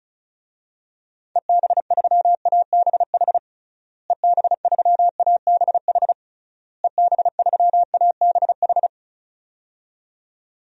Morse code, heard as a keyed tone, about 36 wpm, 700 Hz. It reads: EB3ABH EB3ABH EB3ABH
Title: DXCC-100 Call Signs - 6 Characters - Repeated 3x - 35wpm